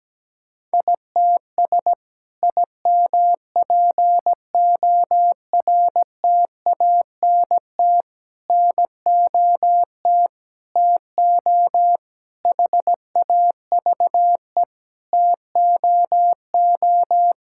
Morse code, heard as a keyed tone, about 17 wpm, 700 Hz. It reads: ITS IMPORTANT NOT TO HAVE TOO
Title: Sentences from Top 1000 Words - Review Part 5 - 17wpm